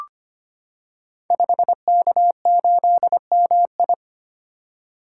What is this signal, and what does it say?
Morse code, heard as a keyed tone, about 25 wpm, 700 Hz. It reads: 5K8MI
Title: Code Groups - 5 Letter:Numbers - 25wpm